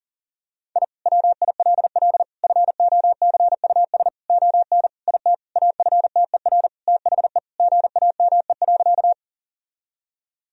Morse code, heard as a keyed tone, about 37 wpm, 700 Hz. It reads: I WILL FOCUS ON IT AFTER THE GAME.
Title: Sentences from Top 1000 Words - Review Part 4 - 40wpm